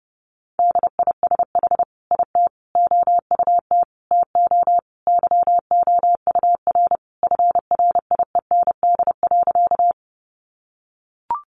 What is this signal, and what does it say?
Morse code, heard as a keyed tone, about 30 wpm, 700 Hz. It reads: DISH IT OUT TO YOUR FRIEND.
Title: Sentences from Top 1300 Words 30wpm